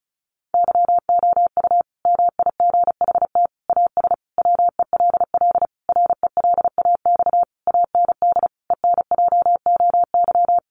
Morse code, heard as a keyed tone, about 34 wpm, 700 Hz. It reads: YOU MIGHT AS WELL RELAX AND ENJOY
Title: Sentences from Top 1400 Words 35wpm